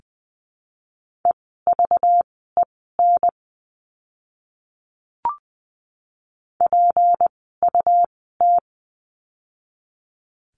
Morse code, heard as a keyed tone, about 20 wpm, spaced as for 14 wpm, 700 Hz. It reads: EVEN PUT
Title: 244 Lesson 39f - Letter z - Words 20wpm